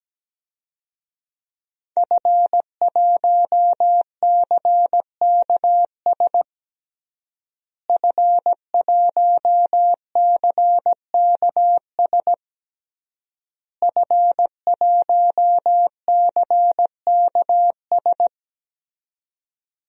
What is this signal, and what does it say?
Morse code, heard as a keyed tone, about 17 wpm, 700 Hz. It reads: F1CKS F1CKS F1CKS
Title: DXCC-100 Call Signs - 5 Characters - Repeated 3x - 17wpm